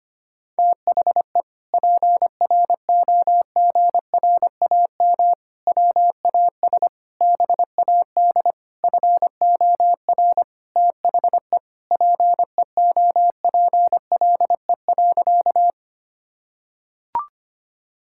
Morse code, heard as a keyed tone, about 25 wpm, 700 Hz. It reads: THE PROGRAM WAS BAD FOR THE PEOPLE.
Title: Sentences from Top 1000 Words - Review Part 5 - 25wpm